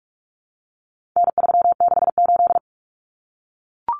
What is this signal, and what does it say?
Morse code, heard as a keyed tone, about 44 wpm, 700 Hz. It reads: N368
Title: Common Cut Numbers - 4-Digit 45wpm